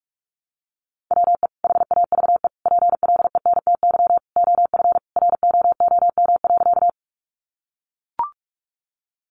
Morse code, heard as a keyed tone, about 44 wpm, 700 Hz. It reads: WE HAVE PLENTY OF ROOM.
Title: Sentences from Top 1300 Words 45wpm